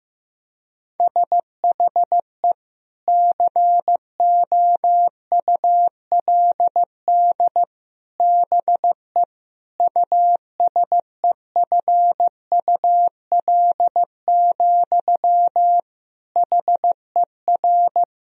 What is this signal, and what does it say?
Morse code, heard as a keyed tone, about 15 wpm, 700 Hz. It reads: SHE COULD BE USEFUL, HER
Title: Sentences from Top 1000 Words - Review Part 4 - 15wpm